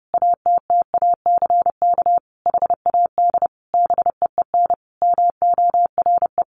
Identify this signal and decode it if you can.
Morse code, heard as a keyed tone, about 30 wpm, 700 Hz. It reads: ATTACK HAD BEEN MORE
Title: Sentences from Top 1500 Words 30wpm